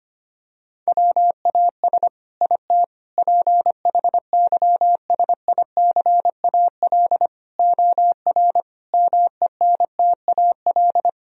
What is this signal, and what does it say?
Morse code, heard as a keyed tone, about 25 wpm, 700 Hz. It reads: WAS IT PHYSICAL OR MENTAL
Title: Sentences from Top 1000 Words - Review Part 4 - 25wpm